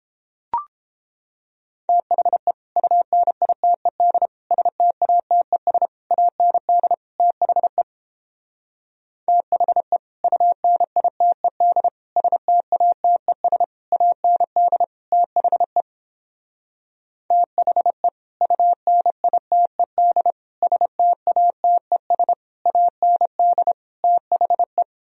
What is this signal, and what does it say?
Morse code, heard as a keyed tone, about 29 wpm, 700 Hz. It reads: THE UNITED STATES AND THE THE UNITED STATES AND THE THE UNITED STATES AND THE